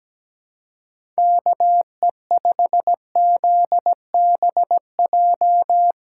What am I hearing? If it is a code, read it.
Morse code, heard as a keyed tone, about 17 wpm, 700 Hz. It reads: KE5ZBJ
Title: USA Callsign DE Callsign 17wpm